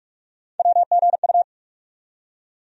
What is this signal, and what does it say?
Morse code, heard as a keyed tone, about 46 wpm, 700 Hz. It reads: WGU
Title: USA Call Sign Postfixes - 3 Letters 45wpm